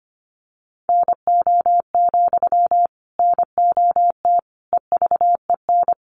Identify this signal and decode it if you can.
Morse code, heard as a keyed tone, about 25 wpm, 700 Hz. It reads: NO, NOT EVEN
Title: Sentences from Top 1300 Words 25wpm